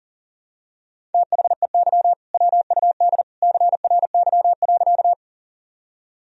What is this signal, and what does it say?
Morse code, heard as a keyed tone, about 37 wpm, 700 Hz. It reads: THEY WUD CRY.
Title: Sentences from Top 500 Words and Abbreviations 40wpm